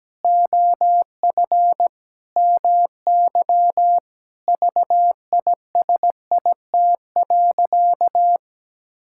Morse code, heard as a keyed tone, about 17 wpm, 700 Hz. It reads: OF MY VISIT.
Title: Sentences from Top 1000 Words - Review Part 6 - 17wpm